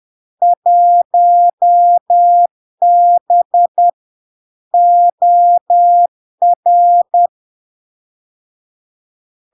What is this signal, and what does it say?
Morse code, heard as a keyed tone, about 10 wpm, 700 Hz. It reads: 1B OR
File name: LARK-FD-2022-Log-in-CW-10wpm-700-Hz